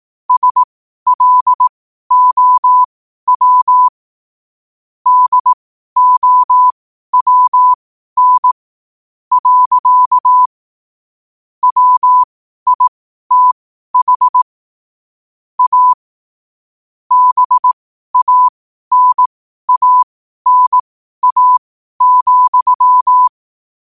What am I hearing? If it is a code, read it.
Morse code, heard as a keyed tone, about 18 wpm, spaced as for 12 wpm, 1000 Hz. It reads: SLOW DOWN. WITH A BANANA,
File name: qotdfiles / 12wpm